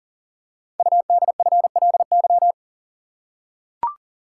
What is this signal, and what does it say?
Morse code, heard as a keyed tone, about 39 wpm, 700 Hz. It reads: UDFLY
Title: Code Groups - 5 Letters - 40wpm